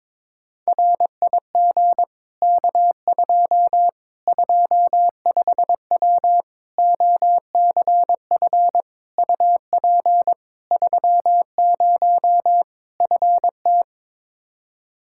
Morse code, heard as a keyed tone, about 22 wpm, 700 Hz. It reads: RIG K2 25W OCF UP 30 FT
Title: Practice QSO - Exchange 2 - Info - 22wpm